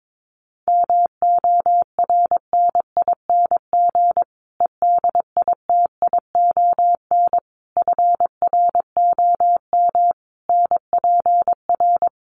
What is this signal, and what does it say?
Morse code, heard as a keyed tone, about 22 wpm, 700 Hz. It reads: MORNING EDITION FROM NPR
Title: Sets of 5 Words Encore 22wpm